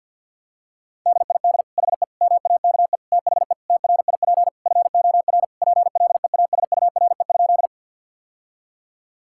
Morse code, heard as a keyed tone, about 50 wpm, 700 Hz. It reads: DID HE MAKE THE TRIP FOR PLEASURE?
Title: Sentences from Top 1600 Words 50wpm